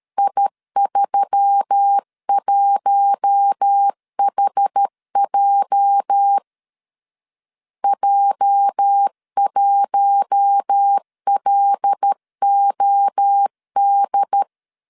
Morse code, heard as a keyed tone, about 13 wpm, 800 Hz. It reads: I31HJ J1LOD
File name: LEZIONI / 8 YJ45 / 8 YJ 12wpm 2235